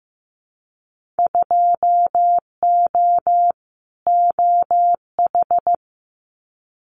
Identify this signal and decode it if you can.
Morse code, heard as a keyed tone, about 15 wpm, 700 Hz. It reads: 2O OH
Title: Winter Field Day Exchanges 15wpm